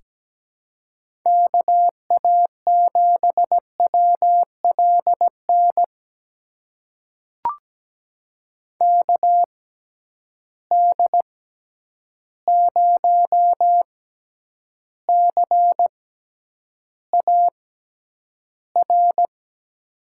Morse code, heard as a keyed tone, about 17 wpm, spaced as for 9 wpm, 700 Hz. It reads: KA7WLN KD0CAR
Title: USA Call Signs - 6x Character Spacing - 17wpm